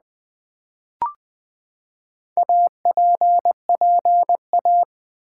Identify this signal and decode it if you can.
Morse code, heard as a keyed tone, about 20 wpm, 700 Hz. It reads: APPA
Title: Build A Word - 4 Letters At A Time - 10-Letter Words - 20wpm